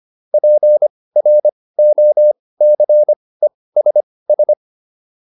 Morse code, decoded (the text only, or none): PROCESS